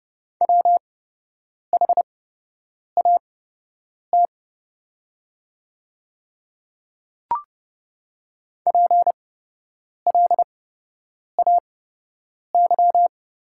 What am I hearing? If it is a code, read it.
Morse code, heard as a keyed tone, about 30 wpm, spaced as for 8 wpm, 700 Hz. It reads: WHAT PLAY
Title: 8x Character Spacing - Top 200 Words - 30wpm